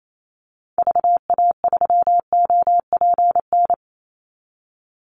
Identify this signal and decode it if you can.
Morse code, heard as a keyed tone, about 28 wpm, 700 Hz. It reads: VA3OPN
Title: Call Signs Repeated 3x - 28wpm